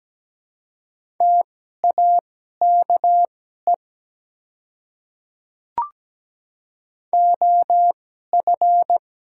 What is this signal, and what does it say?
Morse code, heard as a keyed tone, about 17 wpm, spaced as for 12 wpm, 700 Hz. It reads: TAKE OF
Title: 2x Character Spacing - Top 200 Words - 17wpm